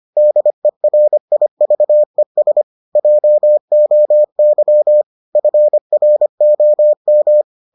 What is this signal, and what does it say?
Morse code, heard as a keyed tone, about 25 wpm, 600 Hz. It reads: DERIVES JOY FROM